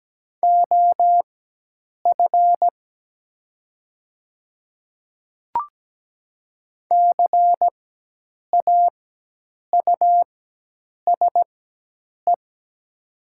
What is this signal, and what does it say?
Morse code, heard as a keyed tone, about 17 wpm, spaced as for 8 wpm, 700 Hz. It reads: OF CAUSE
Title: 4x Character Spacing - Top 300 Words - 17wpm